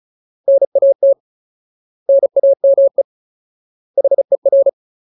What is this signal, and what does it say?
Morse code, heard as a keyed tone, about 34 wpm, 550 Hz. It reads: NAT NAME HER
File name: Ultimate Word List / 35WPM Ultimate